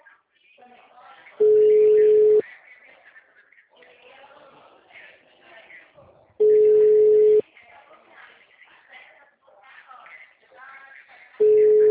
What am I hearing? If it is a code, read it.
Ringback tone (Europe, 425 Hz), with speech.